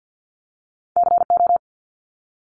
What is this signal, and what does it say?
Morse code, heard as a keyed tone, about 48 wpm, 700 Hz. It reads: CO
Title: ARRL Sections 50wpm